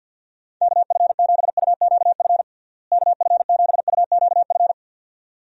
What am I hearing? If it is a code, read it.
Morse code, heard as a keyed tone, about 46 wpm, 700 Hz. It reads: KF7UQF KF7UQF
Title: USA Call Signs - 6 Characters - Repeated 2x - 50wpm